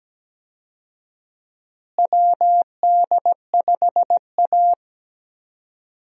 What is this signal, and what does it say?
Morse code, heard as a keyed tone, about 17 wpm, 700 Hz. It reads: WD5A
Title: USA Call Signs - 4 Characters - Repeated 3x - 17wpm